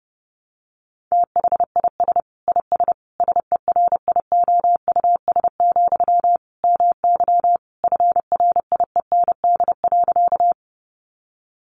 Morse code, heard as a keyed tone, about 30 wpm, 700 Hz. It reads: THIS IS SERIOUS, MY FRIEND.
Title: Sentences from Top 1100 Words 30wpm